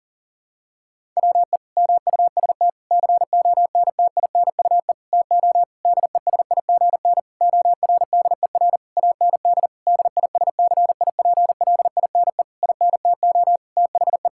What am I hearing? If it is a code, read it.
Morse code, heard as a keyed tone, about 40 wpm, 700 Hz. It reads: WE MUST CONTINUE TO DESIGN ORDER AND DISCIPLINE INTO THE
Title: Sentences from Top 1000 Words - Review Part 6 - 40wpm